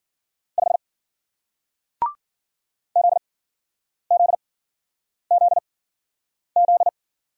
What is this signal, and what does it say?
Morse code, heard as a keyed tone, about 54 wpm, spaced as for 9 wpm, 700 Hz. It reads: 5 ZZZZ